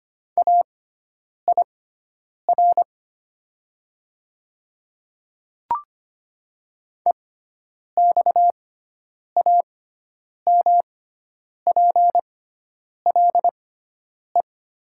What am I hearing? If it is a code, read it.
Morse code, heard as a keyed tone, about 25 wpm, spaced as for 9 wpm, 700 Hz. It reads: AIR EXAMPLE
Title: 6x Character Spacing - Top 300 Words - 25wpm